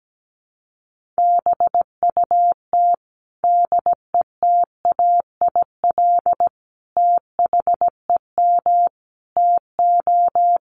Morse code, heard as a keyed tone, about 17 wpm, 700 Hz. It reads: BUT DETAIL THEM TO
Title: Sentences from Top 1500 Words 17wpm